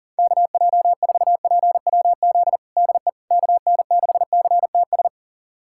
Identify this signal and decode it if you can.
Morse code, heard as a keyed tone, about 37 wpm, 700 Hz. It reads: KJ4PWZ DE KN6CTS
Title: USA Callsign DE Callsign 40wpm